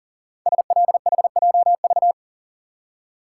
Morse code, heard as a keyed tone, about 42 wpm, 700 Hz. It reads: SLHJV
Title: Code Groups - 5 Letters - 40wpm